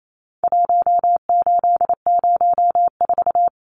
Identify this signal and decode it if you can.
Morse code, heard as a keyed tone, about 28 wpm, 700 Hz. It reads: 1804
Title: Numbers - Contest Style - 4-Digit 28wpm